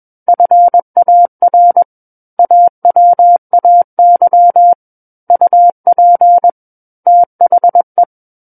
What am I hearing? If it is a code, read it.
Morse code, heard as a keyed tone, about 21 wpm, 700 Hz. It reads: FAR AWAY UP THE
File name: Book One / 12-Chapter 12 - 21 WPM